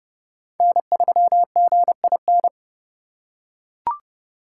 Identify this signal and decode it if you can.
Morse code, heard as a keyed tone, about 30 wpm, 700 Hz. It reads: N3GIN
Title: USA Call Signs - 5 Characters - 4x Character Spacing - 30wpm